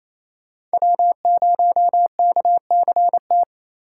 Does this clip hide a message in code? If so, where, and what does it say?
Yes: Morse code from 0.7 to 3.4 s, heard as a keyed tone, about 35 wpm, 700 Hz. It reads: W0KCT